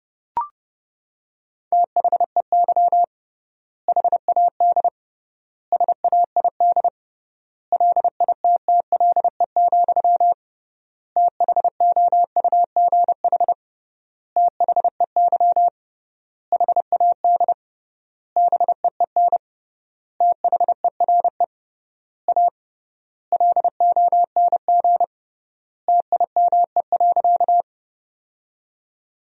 Morse code, heard as a keyed tone, about 30 wpm, 700 Hz. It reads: THEY HAD SAID LITTLE, THOUGH THEY HAD BEEN THERE A LONG TIME.
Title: 3x Word-Spacing - Sentences from Top 400 Words - 30wpm